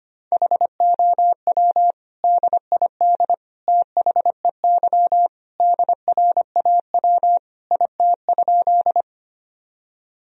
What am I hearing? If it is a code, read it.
Morse code, heard as a keyed tone, about 25 wpm, 700 Hz. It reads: HOW DID THEY DRAW IT?
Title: Sentences from Top 1000 Words - Review Part 1 - 25wpm